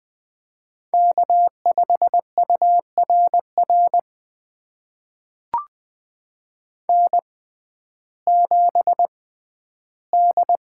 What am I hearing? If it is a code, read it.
Morse code, heard as a keyed tone, about 20 wpm, spaced as for 12 wpm, 700 Hz. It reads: K5URR N7D
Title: USA Call Signs - 5 Characters - 6x Character Spacing - 20wpm